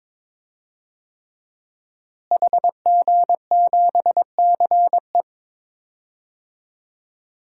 Morse code, heard as a keyed tone, about 21 wpm, 700 Hz. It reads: HG7CE
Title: DXCC-100 Call Signs - 5 Characters - Repeated 3x - 22wpm